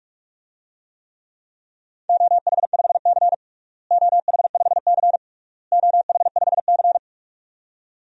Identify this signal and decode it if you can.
Morse code, heard as a keyed tone, about 41 wpm, 700 Hz. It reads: OH5C OH5C OH5C
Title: DXCC-100 Call Signs - 4 Characters - Repeated 3x - 45wpm